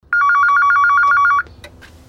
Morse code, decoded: none